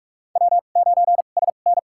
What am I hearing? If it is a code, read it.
Morse code, heard as a keyed tone, about 46 wpm, spaced as for 32 wpm, 700 Hz. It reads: W9SN
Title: USA Call Signs - 4 Characters - 2x Character Spacing - 45wpm